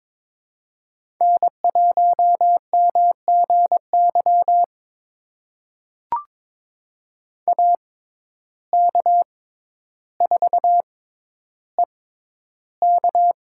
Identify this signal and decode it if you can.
Morse code, heard as a keyed tone, about 22 wpm, spaced as for 11 wpm, 700 Hz. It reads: N1MGY AK4EK
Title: USA Call Signs - 5 Characters - 6x Character Spacing - 22wpm